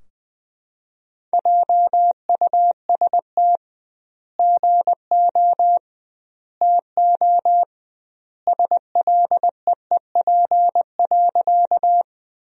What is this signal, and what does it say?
Morse code, heard as a keyed tone, about 20 wpm, 700 Hz. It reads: JUST GO TO SLEEP.
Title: 2x Word-Spacing - Sentences from Top 500 Words - 20wpm